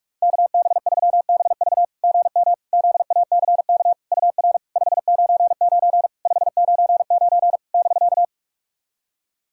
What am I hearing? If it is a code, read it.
Morse code, heard as a keyed tone, about 46 wpm, 700 Hz. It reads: KB3BV GM ZACK UR 599 599 <BK>
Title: SOTA QSO Practice 45wpm